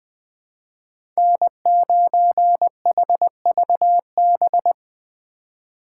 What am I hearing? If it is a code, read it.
Morse code, heard as a keyed tone, about 21 wpm, 700 Hz. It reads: N9HVB
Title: USA Call Signs - 3x Character Spacing - 20wpm